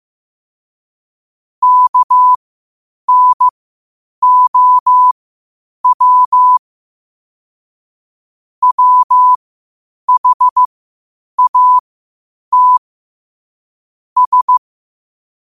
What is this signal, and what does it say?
Morse code, heard as a keyed tone, about 15 wpm, spaced as for 8 wpm, 1000 Hz. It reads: KNOW WHAT S